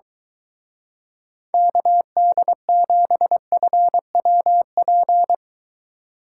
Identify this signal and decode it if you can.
Morse code, heard as a keyed tone, about 23 wpm, 700 Hz. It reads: KD7FWP